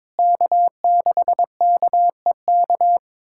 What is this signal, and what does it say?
Morse code, heard as a keyed tone, about 22 wpm, 700 Hz. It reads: K6KEK
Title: USA Callsign DE Callsign - 5 Characters - 22wpm